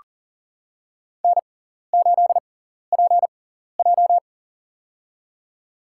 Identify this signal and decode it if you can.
Morse code, heard as a keyed tone, about 40 wpm, spaced as for 14 wpm, 700 Hz. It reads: N8PJ
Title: USA Call Signs - 4 Characters - 6x Character Spacing - 40wpm